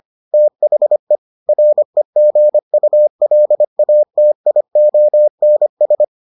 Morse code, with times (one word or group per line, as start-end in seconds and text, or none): 0.3-1.2 THE
1.5-6.0 REGULATIONS